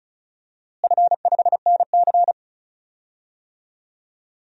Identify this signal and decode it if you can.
Morse code, heard as a keyed tone, about 34 wpm, 700 Hz. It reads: F5NC
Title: DXCC-100 Call Signs - 4 Characters - Repeated 3x - 35wpm